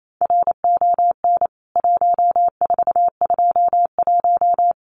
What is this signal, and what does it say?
Morse code, heard as a keyed tone, about 28 wpm, 700 Hz. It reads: RON 1421
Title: CWT Exchange 28wpm